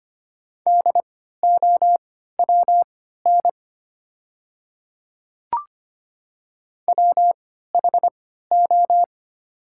Morse code, heard as a keyed tone, about 25 wpm, spaced as for 14 wpm, 700 Hz. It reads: DOWN WHO
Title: 3x Character Spacing - Top 100 Words - 25wpm